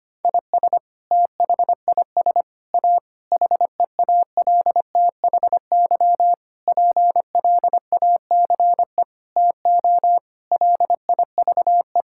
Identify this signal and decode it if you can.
Morse code, heard as a keyed tone, about 25 wpm, 700 Hz. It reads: IS THIS A HEALTHY PLACE TO LIVE
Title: Sentences from Top 1000 Words - Review Part 5 - 25wpm